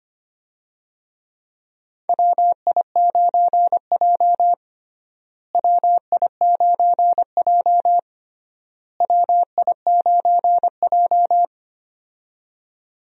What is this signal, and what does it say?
Morse code, heard as a keyed tone, about 25 wpm, 700 Hz. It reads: WI9J WI9J WI9J
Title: USA Call Signs - 4 Characters - Repeated 3x - 25wpm